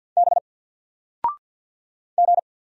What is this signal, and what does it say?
Morse code, heard as a keyed tone, about 50 wpm, 700 Hz. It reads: B G
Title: Single Letter-Number 50wpm